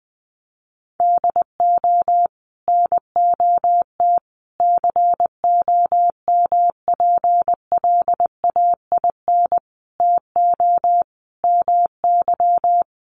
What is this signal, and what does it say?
Morse code, heard as a keyed tone, about 20 wpm, 700 Hz. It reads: DO NOT COMPLAIN TO MY
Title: Sentences from Top 1300 Words 20wpm